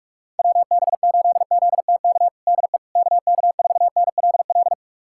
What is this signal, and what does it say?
Morse code, heard as a keyed tone, about 46 wpm, 700 Hz. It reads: WB8ZTK DE KK4NLL
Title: USA Callsign DE Callsign - 6 Characters - 45wpm